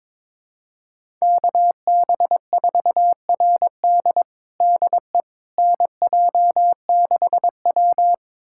Morse code, heard as a keyed tone, about 22 wpm, 700 Hz. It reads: KB4RD DE NJ6W
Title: USA Callsign DE Callsign 22wpm